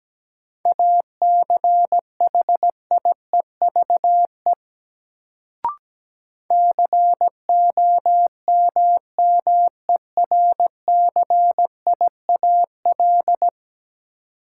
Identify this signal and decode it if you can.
Morse code, heard as a keyed tone, about 17 wpm, 700 Hz. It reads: ACHIEVE COMMERCIAL
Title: Sentences from Top 1000 Words - Review Part 5 - New Word Drill - 17wpm